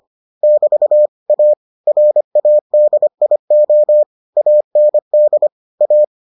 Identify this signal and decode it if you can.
Morse code, heard as a keyed tone, about 25 wpm, 600 Hz. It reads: <BT> A RADIO AND A